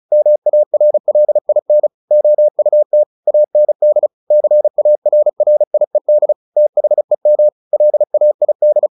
Morse code, heard as a keyed tone, about 34 wpm, 600 Hz. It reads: MARLIN OUT AND CARRIED THEM LAID
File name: the old man and the sea / the old man and the sea 1 35wpm